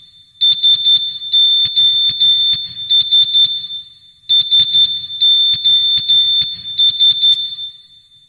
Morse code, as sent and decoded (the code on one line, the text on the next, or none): ... --- ... / ... --- ...
SOS SOS